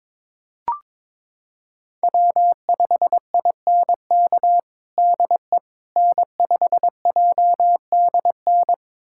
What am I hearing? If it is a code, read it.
Morse code, heard as a keyed tone, about 21 wpm, 700 Hz. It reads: W5INK DE N5JDN
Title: USA Callsign DE Callsign - 5 Characters - 22wpm